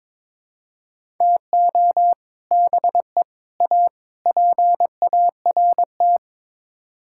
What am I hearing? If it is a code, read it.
Morse code, heard as a keyed tone, about 22 wpm, 700 Hz. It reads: TO BE A PART
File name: practice / 22wpm / Build A Phrase - 5-Word Phrases - 22wpm